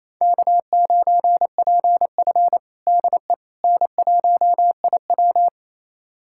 Morse code, heard as a keyed tone, about 28 wpm, 700 Hz. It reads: K9PF DE N1IW
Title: USA Callsign DE Callsign - 4 Characters - 28wpm